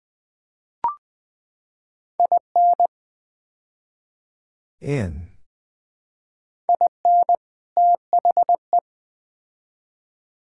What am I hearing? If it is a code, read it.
Morse code, heard as a keyed tone, about 20 wpm, 700 Hz. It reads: IN IN THE
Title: Build A Phrase - 5-Word Phrases - 20wpm